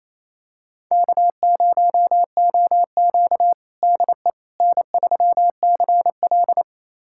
Morse code, heard as a keyed tone, about 28 wpm, 700 Hz. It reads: K0OQ DE N3CL
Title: USA Callsign DE Callsign - 4 Characters - 28wpm